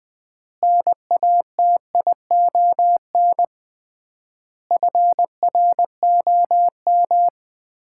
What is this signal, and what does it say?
Morse code, heard as a keyed tone, about 21 wpm, 690 Hz. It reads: NATION FROM
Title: 3x Word-Spacing - Sets of 5 Words - 20wpm